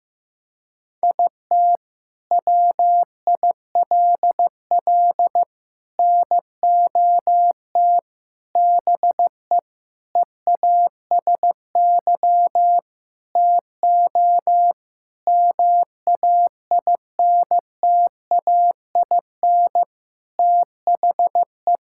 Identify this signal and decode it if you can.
Morse code, heard as a keyed tone, about 15 wpm, 700 Hz. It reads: IT WILL NOT BE EASY TO MAINTAIN THE
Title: Sentences from Top 1000 Words - Review Part 5 - 15wpm